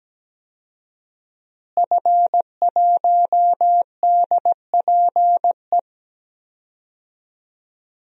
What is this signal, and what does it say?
Morse code, heard as a keyed tone, about 17 wpm, 700 Hz. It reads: F1DPE
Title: DXCC-100 Call Signs - 5 Characters - Repeated 3x - 17wpm